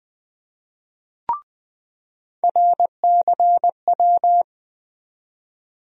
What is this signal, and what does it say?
Morse code, heard as a keyed tone, about 20 wpm, 700 Hz. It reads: RCW
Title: USA Call Sign Postfixes - 3 Letters 20wpm